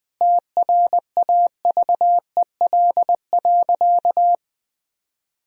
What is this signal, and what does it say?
Morse code, heard as a keyed tone, about 20 wpm, 700 Hz. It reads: TRAVEL.
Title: Sentences from Top 1000 Words - Review Part 2 - 20wpm